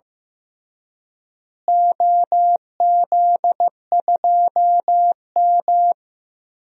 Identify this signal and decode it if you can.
Morse code, heard as a keyed tone, about 15 wpm, 700 Hz. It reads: OZ2M
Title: DXCC-100 Call Signs - 4 Characters - Repeated 2x - 15wpm